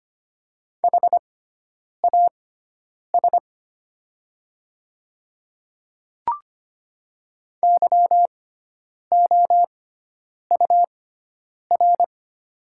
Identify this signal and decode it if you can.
Morse code, heard as a keyed tone, about 25 wpm, spaced as for 9 wpm, 700 Hz. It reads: HAS YOUR
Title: 6x Character Spacing - Top 100 Words - 25wpm